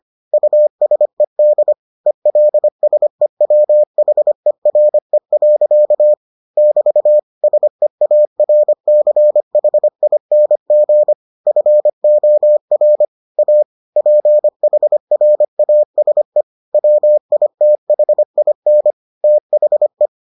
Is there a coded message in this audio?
Yes: Morse code from 0.3 to 20.1 s, heard as a keyed tone, about 25 wpm, 600 Hz. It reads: USED ELSEWHERE. <BT> SEARCHING FOR A PHRASE WITHIN THE